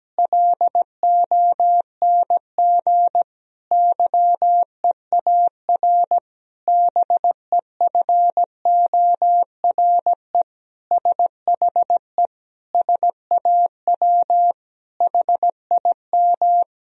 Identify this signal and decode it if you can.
Morse code, heard as a keyed tone, about 17 wpm, 700 Hz. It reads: LONG YEAR BEFORE SHE SAW HIM
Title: Sentences from Top 1000 Words - Review Part 1 - 17wpm